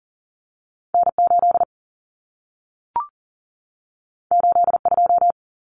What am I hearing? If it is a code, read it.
Morse code, heard as a keyed tone, about 40 wpm, 700 Hz. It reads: N8 82
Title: Common Cut Numbers - 2-Digit 40wpm